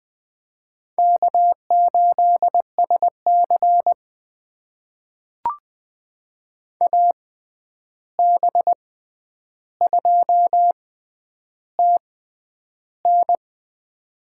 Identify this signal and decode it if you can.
Morse code, heard as a keyed tone, about 21 wpm, spaced as for 10 wpm, 700 Hz. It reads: K8SC AB2TN
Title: USA Call Signs - 6x Character Spacing - 20wpm